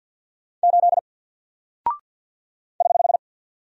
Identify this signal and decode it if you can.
Morse code, heard as a keyed tone, about 50 wpm, 700 Hz. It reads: 8 <HH>
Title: Single Character (Letter:Number:Punctuation:Prosign) 50wpm